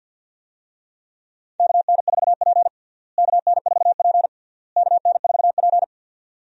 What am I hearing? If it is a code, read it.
Morse code, heard as a keyed tone, about 49 wpm, 700 Hz. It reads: KN4P KN4P KN4P
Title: USA Call Signs - 4 Characters - Repeated 3x - 50wpm